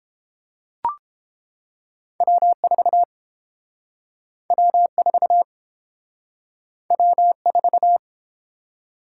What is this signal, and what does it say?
Morse code, heard as a keyed tone, about 32 wpm, 700 Hz. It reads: W4 W4 W4